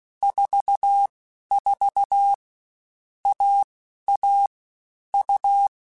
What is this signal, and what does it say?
Morse code, heard as a keyed tone, about 16 wpm, 800 Hz. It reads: 4 4 A A U